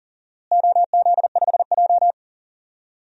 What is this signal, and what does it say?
Morse code, heard as a keyed tone, about 40 wpm, 700 Hz. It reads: OZ5J
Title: DXCC-100 Call Signs - 4 Characters - Repeated 2x - 40wpm